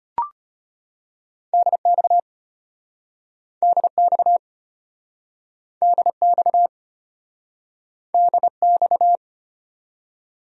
Morse code, heard as a keyed tone, about 33 wpm, 700 Hz. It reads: DX DX DX DX